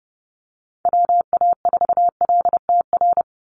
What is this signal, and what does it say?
Morse code, heard as a keyed tone, about 30 wpm, 700 Hz. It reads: WA4LTR
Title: USA Call Signs 30wpm